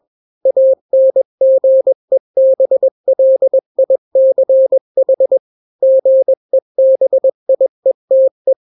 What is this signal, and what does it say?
Morse code, heard as a keyed tone, about 21 wpm, 530 Hz. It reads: ANGEBLICH GEBIETE
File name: daily / 20wpm